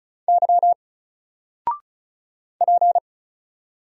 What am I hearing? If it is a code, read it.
Morse code, heard as a keyed tone, about 35 wpm, 700 Hz. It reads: Y P
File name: Single-Letter-Number-35wpm